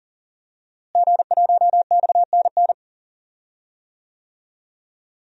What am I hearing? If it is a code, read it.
Morse code, heard as a keyed tone, about 38 wpm, 700 Hz. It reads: G1XNN
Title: DXCC-100 Call Signs - 5 Characters - Repeated 2x - 40wpm